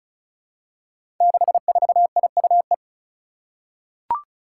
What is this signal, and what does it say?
Morse code, heard as a keyed tone, about 36 wpm, 700 Hz. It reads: 64IUE
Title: Five Challenging Letter-Number Combinations 35wpm